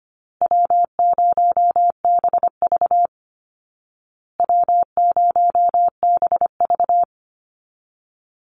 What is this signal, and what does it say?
Morse code, heard as a keyed tone, about 25 wpm, 700 Hz. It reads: W0BV W0BV
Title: Easy Call Signs Repeated 3x - 25wpm